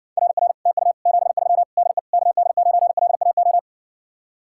Morse code, heard as a keyed tone, about 60 wpm, 700 Hz. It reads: RR TU 73 DE KD9LAO